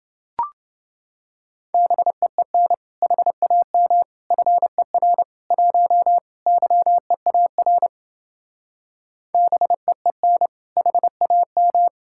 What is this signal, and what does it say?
Morse code, heard as a keyed tone, about 28 wpm, 700 Hz. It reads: BEEN HAM FER 1 YEAR BEEN HAM